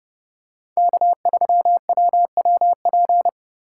Morse code, heard as a keyed tone, about 30 wpm, 700 Hz. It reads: K3WWP
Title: K1USN SST QSO Practice - 30wpm